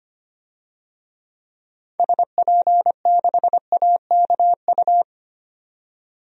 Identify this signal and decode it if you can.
Morse code, heard as a keyed tone, about 25 wpm, 700 Hz. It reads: SP6AKU
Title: DXCC-100 Call Signs - 6 Characters - 25wpm